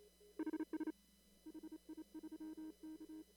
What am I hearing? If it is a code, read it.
Morse code, heard as a keyed tone, about 33 wpm, 310 Hz. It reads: HS HI3K
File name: CW / kt5j wpx cw-op-k5pi